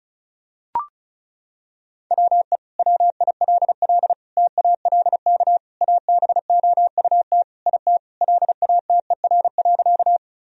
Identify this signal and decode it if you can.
Morse code, heard as a keyed tone, about 34 wpm, 700 Hz. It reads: WE WILL TALK ABOUT IT LATER.
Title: Sentences from Top 1000 Words - Review Part 4 - 35wpm